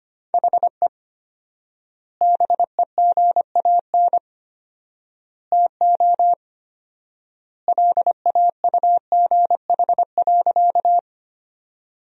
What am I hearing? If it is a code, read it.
Morse code, heard as a keyed tone, about 25 wpm, 700 Hz. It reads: HE BEGAN TO LAUGH.
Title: 4x Word-Spacing - Sentences from Top 500 Words - 25wpm